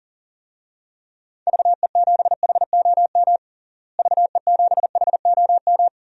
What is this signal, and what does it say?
Morse code, heard as a keyed tone, about 40 wpm, 700 Hz. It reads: VE7HOM VE7HOM
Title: DXCC-100 Call Signs - 6 Characters - Repeated 3x - 40wpm